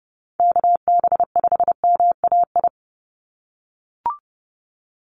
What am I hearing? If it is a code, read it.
Morse code, heard as a keyed tone, about 30 wpm, 700 Hz. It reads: KB5MAI
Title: USA Call Signs 30wpm